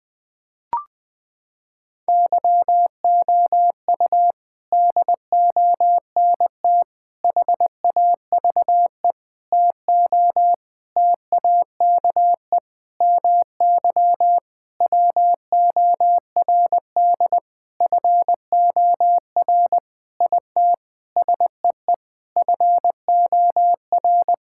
Morse code, heard as a keyed tone, about 21 wpm, 700 Hz. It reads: YOU DONT HAVE TO TAKE MY WORD FOR IT SEE FOR
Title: Sentences with 8-Word Phrases - 20wpm